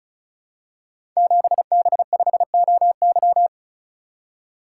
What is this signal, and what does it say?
Morse code, heard as a keyed tone, about 36 wpm, 700 Hz. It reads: 7B5OY
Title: Code Groups - 5 Letter:Numbers - 35wpm